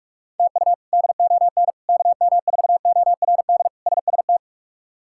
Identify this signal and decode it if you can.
Morse code, heard as a keyed tone, about 44 wpm, 700 Hz. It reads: TU DON KM4ORD SST
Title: K1USN SST QSO Practice - 45wpm